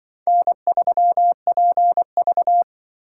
Morse code, heard as a keyed tone, about 24 wpm, 700 Hz. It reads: N3PV
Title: Speed Racing - USA Call Signs 20wpm